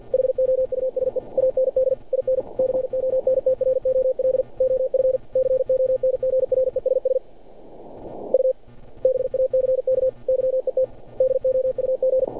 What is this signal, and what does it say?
Morse code, heard as a keyed tone, about 47 wpm, 520 Hz. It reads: HOUSE AND IN CONTROL OF CONGRESS U BACK YET DOUG
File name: archived / CRSnet